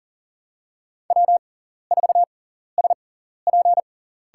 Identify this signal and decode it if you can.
Morse code, heard as a keyed tone, about 40 wpm, spaced as for 14 wpm, 700 Hz. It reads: W4SP
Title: USA Call Signs - 6x Character Spacing - 40wpm